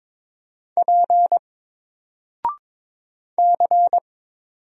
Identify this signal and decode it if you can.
Morse code, heard as a keyed tone, about 22 wpm, 700 Hz. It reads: P C
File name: Single-Letter-Number-22wpm